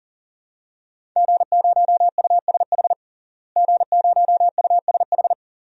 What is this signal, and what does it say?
Morse code, heard as a keyed tone, about 37 wpm, 700 Hz. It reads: G0USH G0USH
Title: DXCC-100 Call Signs - 5 Characters - Repeated 2x - 40wpm